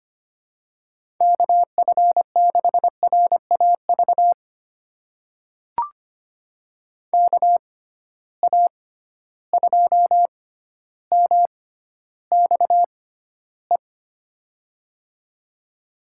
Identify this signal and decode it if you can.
Morse code, heard as a keyed tone, about 25 wpm, spaced as for 13 wpm, 700 Hz. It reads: KF6RAV KA2MXE
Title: USA Call Signs - 6x Character Spacing - 25wpm